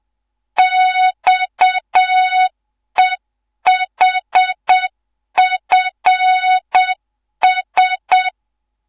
Morse code, heard as a keyed tone, about 7 wpm, 3700 Hz. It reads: XEHFS